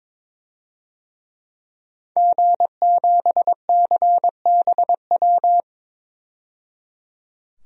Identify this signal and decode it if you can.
Morse code, heard as a keyed tone, about 22 wpm, 700 Hz. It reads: G7CBW
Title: DXCC-100 Call Signs - 5 Characters - 22wpm